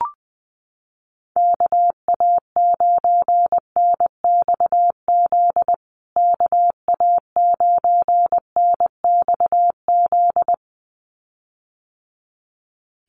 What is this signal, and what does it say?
Morse code, heard as a keyed tone, about 20 wpm, 700 Hz. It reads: KA9NXZ KA9NXZ
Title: USA Call Signs Repeated 2x 20wpm